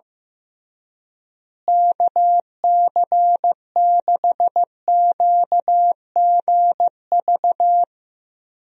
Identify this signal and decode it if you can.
Morse code, heard as a keyed tone, about 15 wpm, 700 Hz. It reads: KC6QGV
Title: USA Call Signs - 6 Characters - Repeated 3x - 15wpm